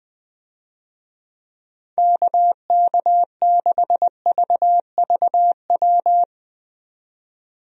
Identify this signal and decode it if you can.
Morse code, heard as a keyed tone, about 20 wpm, 700 Hz. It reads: KK6VVW
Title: USA Call Signs - 6 Characters - 20wpm